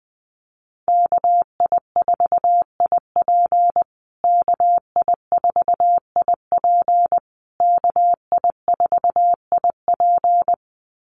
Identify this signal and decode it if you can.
Morse code, heard as a keyed tone, about 20 wpm, 700 Hz. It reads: KI4IP KI4IP KI4IP
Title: USA Call Signs Repeated 3x 20wpm